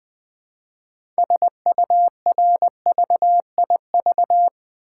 Morse code, heard as a keyed tone, about 20 wpm, 700 Hz. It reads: SURVIV
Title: Build A Word - 2 Letters At A Time - 8-Letter Words - 20wpm